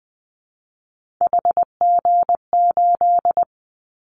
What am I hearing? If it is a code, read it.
Morse code, heard as a keyed tone, about 19 wpm, 700 Hz. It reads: HG8
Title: Callsign Prefixes for Top 50 DXCC - 20wpm